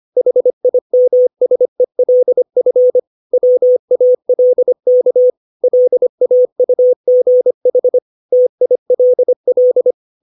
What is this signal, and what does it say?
Morse code, heard as a keyed tone, about 25 wpm, 500 Hz. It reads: HIMSELF WALK LAUGH TILL
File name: 632 words / 632 Words 25WPM